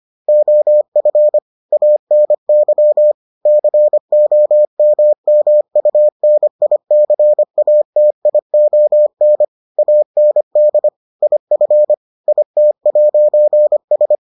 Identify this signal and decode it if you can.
Morse code, heard as a keyed tone, about 25 wpm, 600 Hz. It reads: OF ANY COMMUNICATION AND IF IT'S